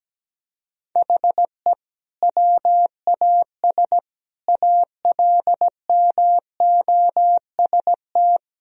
Morse code, heard as a keyed tone, about 17 wpm, 700 Hz. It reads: HE WAS ALMOST
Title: Sentences from Top 1000 Words - Review Part 4 - 17wpm